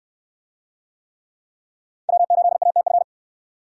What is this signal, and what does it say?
Morse code, heard as a keyed tone, about 61 wpm, 700 Hz. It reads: W8ATF